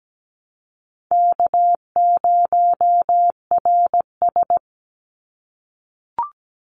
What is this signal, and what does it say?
Morse code, heard as a keyed tone, about 17 wpm, 700 Hz. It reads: K0RS
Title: NA SOTA Chaser Callsigns 17wpm